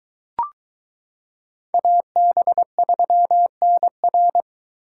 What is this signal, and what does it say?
Morse code, heard as a keyed tone, about 21 wpm, 700 Hz. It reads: AB3NR